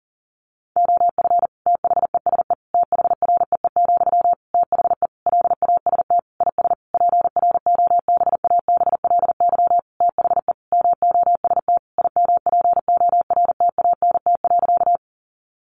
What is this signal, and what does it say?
Morse code, heard as a keyed tone, about 40 wpm, 700 Hz. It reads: OF THESE THREE, THE LAST IS PROBABLY THE MOST IMPORTANT.
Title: Sentences from Top 700 Words 40wpm